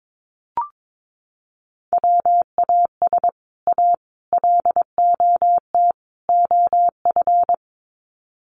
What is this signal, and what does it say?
Morse code, heard as a keyed tone, about 22 wpm, 700 Hz. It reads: WAS A LOT OF
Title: Sets of 4 Words 22wpm